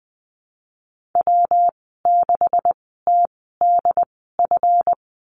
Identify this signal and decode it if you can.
Morse code, heard as a keyed tone, about 20 wpm, spaced as for 15 wpm, 700 Hz. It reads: W6TDF
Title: 179 Lesson 29g - Slash - Call Signs 20wpm